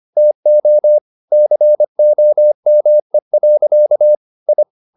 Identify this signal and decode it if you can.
Morse code, heard as a keyed tone, about 25 wpm, 600 Hz. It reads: TO COME. I